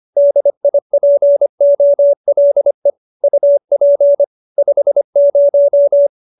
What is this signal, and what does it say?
Morse code, heard as a keyed tone, about 25 wpm, 570 Hz. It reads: DIPOLE UP 50